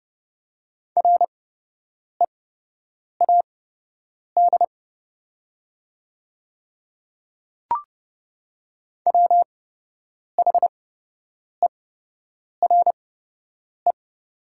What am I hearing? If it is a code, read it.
Morse code, heard as a keyed tone, about 30 wpm, spaced as for 8 wpm, 700 Hz. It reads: READ WHERE
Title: 8x Character Spacing - Top 300 Words - 30wpm